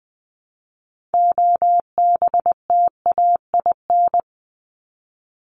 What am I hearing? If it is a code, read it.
Morse code, heard as a keyed tone, about 20 wpm, 700 Hz. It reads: OBTAIN
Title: ED Suffix - Top 100 Words with Base Words 20wpm